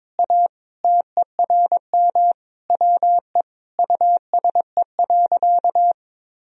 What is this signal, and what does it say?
Morse code, heard as a keyed tone, about 22 wpm, 700 Hz. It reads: A TERM WE USE.
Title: Sentences from Top 1000 Words - Review Part 4 - 22wpm